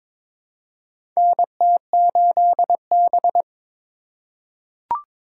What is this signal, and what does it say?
Morse code, heard as a keyed tone, about 22 wpm, 700 Hz. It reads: NT8B
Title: USA Call Signs - 2x Character Spacing - 22wpm